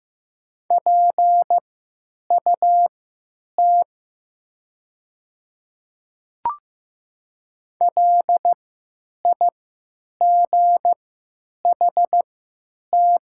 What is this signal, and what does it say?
Morse code, heard as a keyed tone, about 15 wpm, spaced as for 9 wpm, 700 Hz. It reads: PUT LIGHT
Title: 3x Character Spacing - Top 300 Words - 15wpm